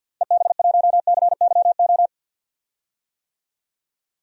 Morse code, heard as a keyed tone, about 50 wpm, 700 Hz. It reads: EB1CYO
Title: DXCC-100 Call Signs - 6 Characters - 50wpm